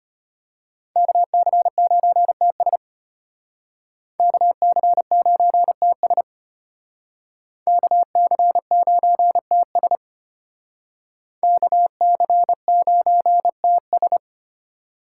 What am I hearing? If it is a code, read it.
Morse code, heard as a keyed tone, about 33 wpm, 700 Hz. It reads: KC9TS KC9TS KC9TS KC9TS